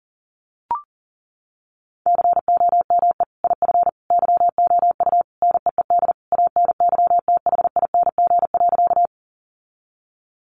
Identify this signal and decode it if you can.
Morse code, heard as a keyed tone, about 40 wpm, 700 Hz. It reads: COME IF YOU NEED ANYTHING.
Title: Sentences from Top 1000 Words 40wpm